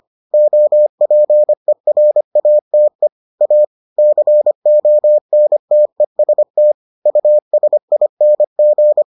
Morse code, heard as a keyed tone, about 25 wpm, 600 Hz. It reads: OPERATE A CONTEST USING